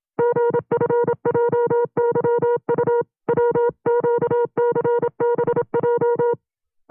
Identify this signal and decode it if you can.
Morse code, heard as a keyed tone, about 27 wpm, 480 Hz. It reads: GFJYU WQCBJ
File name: CW-practice/25WPM